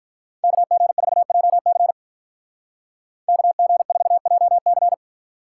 Morse code, heard as a keyed tone, about 47 wpm, 700 Hz. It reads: KG4JC KG4JC